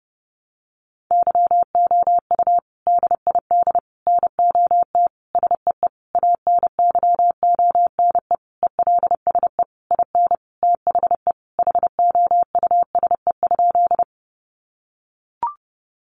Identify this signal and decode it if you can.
Morse code, heard as a keyed tone, about 30 wpm, 700 Hz. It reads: YOU DID NOT SEE ANYONE ELSE IN THE HOUSE?
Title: Sentences from Top 900 Words 30wpm